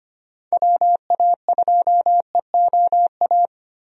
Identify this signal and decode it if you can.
Morse code, heard as a keyed tone, about 25 wpm, 700 Hz. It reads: WA2EOA
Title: USA Call Signs - 6 Characters - Repeated 3x - 25wpm